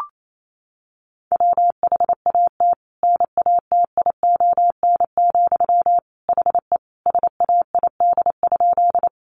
Morse code, heard as a keyed tone, about 28 wpm, 700 Hz. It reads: WHAT NATION, HE SAID?
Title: Sentences from Top 900 Words 28wpm